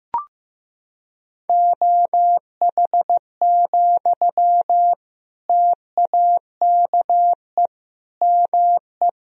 Morse code, heard as a keyed tone, about 15 wpm, 700 Hz. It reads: OH, TAKE ME
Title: Sentences from Top 500 Words and Abbreviations 15wpm